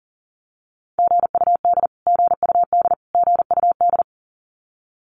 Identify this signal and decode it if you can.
Morse code, heard as a keyed tone, about 40 wpm, 700 Hz. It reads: GUD GUD GUD
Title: Abbreviations Repeated 3x - 40wpm